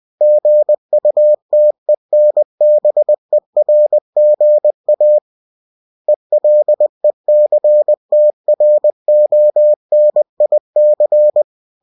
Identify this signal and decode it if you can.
Morse code, heard as a keyed tone, about 20 wpm, 600 Hz. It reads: GUTENBERGA ELECTRONIC
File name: cw-books / winnie-the-pooh / winnie-20wpm-page-0113